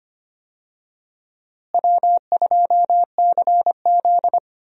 Morse code, heard as a keyed tone, about 25 wpm, 700 Hz. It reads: W2CZ
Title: USA Call Signs - 4 Characters - Repeated 2x - 25wpm